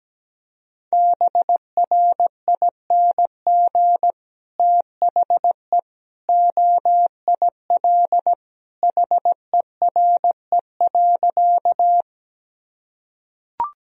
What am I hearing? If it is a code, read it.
Morse code, heard as a keyed tone, about 17 wpm, 700 Hz. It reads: BRING THE OIL HERE.
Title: Sentences from Top 1000 Words - Review Part 4 - 17wpm